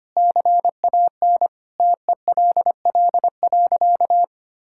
Morse code, heard as a keyed tone, about 25 wpm, 700 Hz. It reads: CAN TELL.
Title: Sentences from Top 1600 Words 25wpm